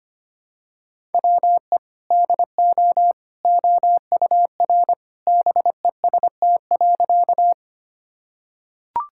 Morse code, heard as a keyed tone, about 25 wpm, 700 Hz. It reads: WE DO OUR BEST.
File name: practice / 25wpm / Sentences with 2-Word Phrases - 25wpm